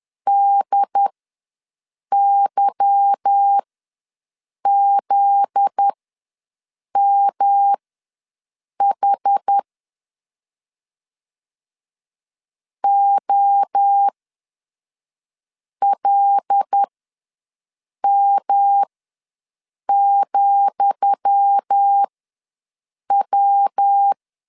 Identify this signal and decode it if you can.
Morse code, heard as a keyed tone, about 10 wpm, spaced as for 6 wpm, 800 Hz. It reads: DYZMH OLM,W
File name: LEZIONI / 10 SIGNS / 10D COMA ws 10wpm 4314